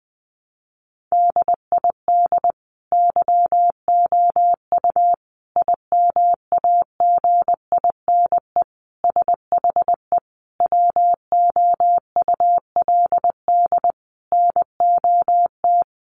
Morse code, heard as a keyed tone, about 20 wpm, 700 Hz. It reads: DID YOU IMAGINE SHE WOULD NOT
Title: Sentences from Top 1400 Words 20wpm